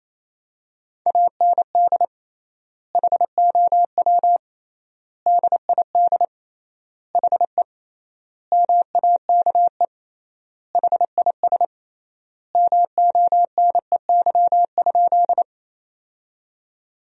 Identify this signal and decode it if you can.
Morse code, heard as a keyed tone, about 28 wpm, 700 Hz. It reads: AND HOW DID HE MAKE HIS MONEY?
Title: 3x Word-Spacing - Sentences from Top 400 Words - 28wpm